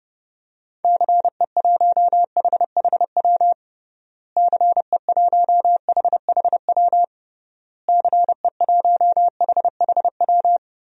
Morse code, heard as a keyed tone, about 30 wpm, 700 Hz. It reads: CE1HHW CE1HHW CE1HHW
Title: DXCC-100 Call Signs - 6 Characters - Repeated 3x - 30wpm